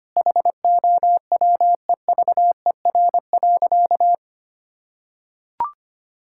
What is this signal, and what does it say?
Morse code, heard as a keyed tone, about 25 wpm, 700 Hz. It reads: HOWEVER.
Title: Sentences from Top 1000 Words - Review Part 6 - 25wpm